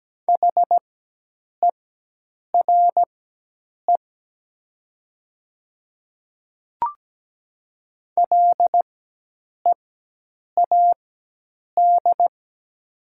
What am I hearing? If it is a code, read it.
Morse code, heard as a keyed tone, about 17 wpm, spaced as for 8 wpm, 700 Hz. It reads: HERE LEAD
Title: 4x Character Spacing - Top 500 Words - 17wpm